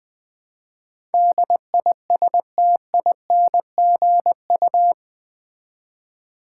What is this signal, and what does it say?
Morse code, heard as a keyed tone, about 20 wpm, 700 Hz. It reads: DISTINGU
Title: Build A Word - 4 Letters At A Time - 11-Letter Words - 20wpm